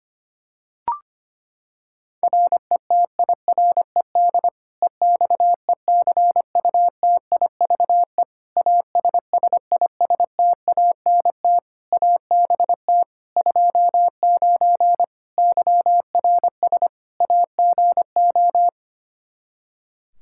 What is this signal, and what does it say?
Morse code, heard as a keyed tone, about 25 wpm, 700 Hz. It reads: RETIRED EXECUTIVE ASSISTANT ABT 29 YRS AGO
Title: Practice QSO - Exchange 3 - More Info - 25wpm